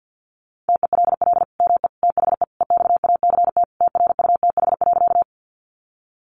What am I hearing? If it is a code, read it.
Morse code, heard as a keyed tone, about 50 wpm, 700 Hz. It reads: TELL ME THE EXACT TRUTH.
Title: Sentences from Top 1300 Words 50wpm